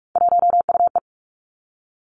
Morse code, heard as a keyed tone, about 46 wpm, 700 Hz. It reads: 1UE
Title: Three Challenging Letter-Number Combinations 45wpm